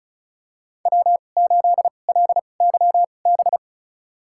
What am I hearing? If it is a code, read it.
Morse code, heard as a keyed tone, about 36 wpm, spaced as for 25 wpm, 700 Hz. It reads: W8LYB
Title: USA Call Signs - 2x Character Spacing - 35wpm